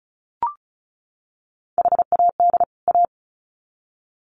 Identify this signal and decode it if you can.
Morse code, heard as a keyed tone, about 34 wpm, 700 Hz. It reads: HAD A
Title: Sets of 2 Words 35wpm